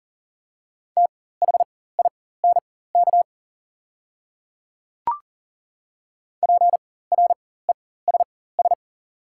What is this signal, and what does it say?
Morse code, heard as a keyed tone, about 38 wpm, spaced as for 18 wpm, 700 Hz. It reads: THINK PRESS
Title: 4x Character Spacing - Top 500 Words - 40wpm